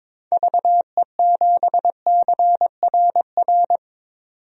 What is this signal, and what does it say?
Morse code, heard as a keyed tone, about 22 wpm, 700 Hz. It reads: VE7CRR
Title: DXCC-100 Call Signs - 6 Characters - 22wpm